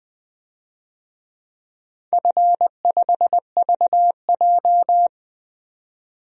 Morse code, heard as a keyed tone, about 21 wpm, 700 Hz. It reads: F5VJ
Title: DXCC-100 Call Signs - 4 Characters - 20wpm